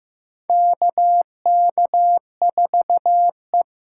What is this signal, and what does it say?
Morse code, heard as a keyed tone, about 15 wpm, 700 Hz. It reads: KK4E
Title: USA Call Signs - 6 Characters - 6x Character Spacing - 15wpm